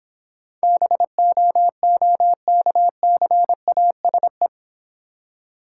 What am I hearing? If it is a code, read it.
Morse code, heard as a keyed tone, about 26 wpm, 700 Hz. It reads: BOOKCASE